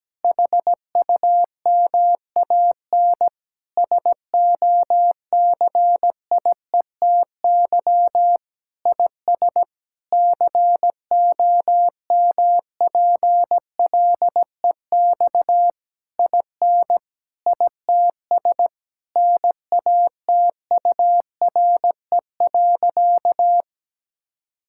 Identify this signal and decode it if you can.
Morse code, heard as a keyed tone, about 17 wpm, 700 Hz. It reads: HUMAN SOCIETY IS COMPLEX IN ITS NATURE.
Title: Sentences from Top 1000 Words - Review Part 6 - 17wpm